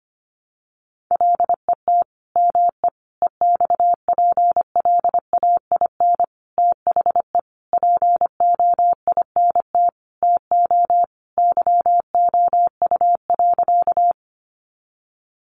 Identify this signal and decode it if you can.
Morse code, heard as a keyed tone, about 25 wpm, 700 Hz. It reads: LET ME EXPLAIN THE POINT TO YOU.
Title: Sentences from Top 1100 Words 25wpm